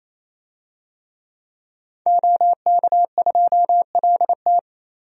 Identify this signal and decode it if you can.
Morse code, heard as a keyed tone, about 28 wpm, 700 Hz. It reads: OK2LT
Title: DXCC-100 Call Signs - 5 Characters - 28wpm